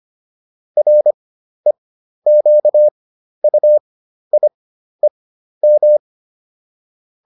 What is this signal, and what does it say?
Morse code, heard as a keyed tone, about 25 wpm, spaced as for 12 wpm, 600 Hz. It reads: REQUIEM